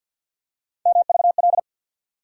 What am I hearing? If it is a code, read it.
Morse code, heard as a keyed tone, about 50 wpm, 700 Hz. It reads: MVL
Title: USA Call Sign Postfixes - 3 Letters 50wpm